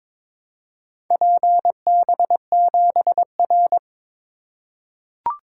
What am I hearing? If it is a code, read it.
Morse code, heard as a keyed tone, about 21 wpm, 700 Hz. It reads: PB7R
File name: practice / 22wpm / Four Letter-Number Combinations - 22wpm